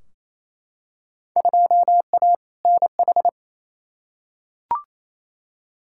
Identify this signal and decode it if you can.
Morse code, heard as a keyed tone, about 28 wpm, 700 Hz. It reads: 2A NH
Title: Field Day Exchanges Encore - 28wpm